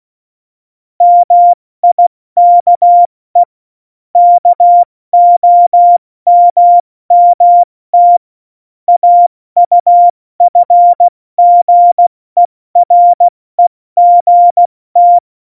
Morse code, heard as a keyed tone, about 16 wpm, 700 Hz. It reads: MIKE KOMMT AUFGEREGT
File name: Morsekurs / Witze / Witz 26. Februar 2019, 70 F80